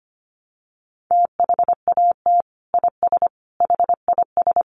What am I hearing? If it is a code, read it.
Morse code, heard as a keyed tone, about 25 wpm, 700 Hz. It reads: THAT IS HIS
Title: Sentences from Top 900 Words 25wpm